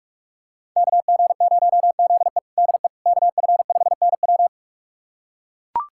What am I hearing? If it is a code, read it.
Morse code, heard as a keyed tone, about 41 wpm, 700 Hz. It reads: KG0ZE DE KF5NW
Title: USA Callsign DE Callsign - 5 Characters - 45wpm